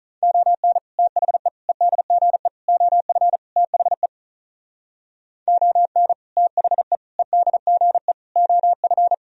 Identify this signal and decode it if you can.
Morse code, heard as a keyed tone, about 43 wpm, 700 Hz. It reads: ON THE EDGE OF THE ON THE EDGE OF
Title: Speed-Racing - Sets of 5 Words 35wpm